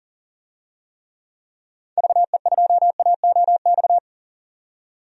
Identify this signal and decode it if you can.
Morse code, heard as a keyed tone, about 40 wpm, 700 Hz. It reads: VE2AOX
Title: DXCC-100 Call Signs - 6 Characters - 40wpm